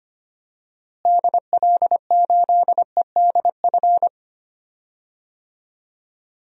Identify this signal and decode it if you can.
Morse code, heard as a keyed tone, about 25 wpm, 700 Hz. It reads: DL8EDF
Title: DXCC-100 Call Signs - 6 Characters - Repeated 3x - 25wpm